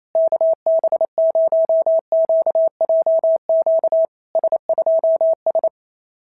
Morse code, heard as a keyed tone, about 28 wpm, 650 Hz. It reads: KB0QJQ S2S
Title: SOTA QSO Practice 28wpm